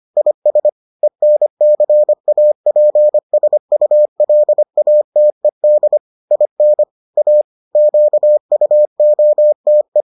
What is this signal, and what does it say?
Morse code, heard as a keyed tone, about 25 wpm, 600 Hz. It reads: IS ENCAPSULATED IN A QUOTE